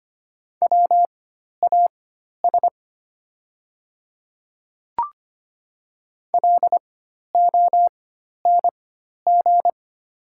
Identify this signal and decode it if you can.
Morse code, heard as a keyed tone, about 25 wpm, spaced as for 12 wpm, 690 Hz. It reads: WAS LONG
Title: 4x Character Spacing - Top 200 Words - 25wpm